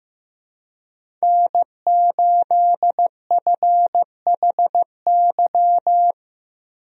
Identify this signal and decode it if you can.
Morse code, heard as a keyed tone, about 15 wpm, 700 Hz. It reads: N8FHY
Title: USA Call Signs - 5 Characters - 15wpm